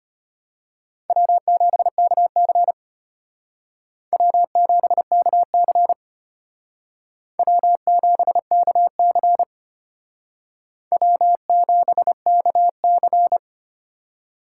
Morse code, heard as a keyed tone, about 32 wpm, 700 Hz. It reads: W7KC W7KC W7KC W7KC